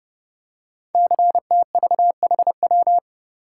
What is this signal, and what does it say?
Morse code, heard as a keyed tone, about 30 wpm, 700 Hz. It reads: CTVHW
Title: Code Groups - 5 Letters - 30wpm